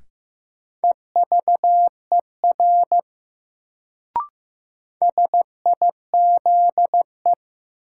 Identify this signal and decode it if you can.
Morse code, heard as a keyed tone, about 15 wpm, 700 Hz. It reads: EVER SIZE
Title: Sentences from Top 400 Words - New Word Drill - 15wpm